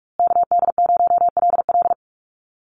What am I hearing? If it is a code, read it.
Morse code, heard as a keyed tone, about 44 wpm, 700 Hz. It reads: KD0LL
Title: USA Call Signs Repeated 3x 45wpm